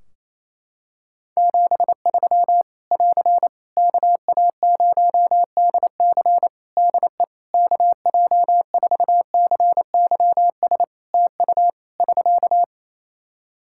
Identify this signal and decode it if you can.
Morse code, heard as a keyed tone, about 28 wpm, 700 Hz. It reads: 73 <AR> KA0DC DE KJ4CYS TU <SK>
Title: K1USN SST QSO Practice 28wpm